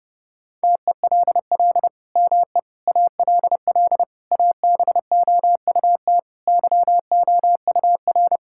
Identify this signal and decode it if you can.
Morse code, heard as a keyed tone, about 30 wpm, 700 Hz. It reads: TELL ME ALL ABOUT YOUR
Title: Sentences from Top 1000 Words - Review Part 6 - 30wpm